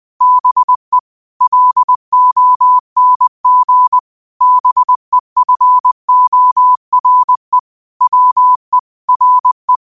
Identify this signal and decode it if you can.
Morse code, heard as a keyed tone, about 20 wpm, 1000 Hz. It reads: BE LONG BEFORE WE RE